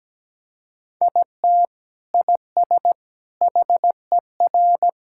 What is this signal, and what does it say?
Morse code, heard as a keyed tone, about 17 wpm, 700 Hz. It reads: IT IS HER
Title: Sentences from Top 1600 Words 17wpm